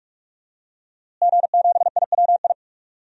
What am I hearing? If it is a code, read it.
Morse code, heard as a keyed tone, about 46 wpm, 700 Hz. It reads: G7IWI
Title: DXCC-100 Call Signs - 5 Characters - Repeated 3x - 45wpm